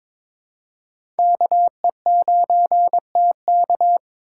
Morse code, heard as a keyed tone, about 22 wpm, 700 Hz. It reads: KE9TK
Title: USA Call Signs - 2x Character Spacing - 22wpm